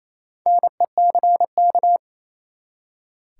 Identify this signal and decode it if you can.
Morse code, heard as a keyed tone, about 28 wpm, 700 Hz. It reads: NECK
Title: Four Letter Words - 28wpm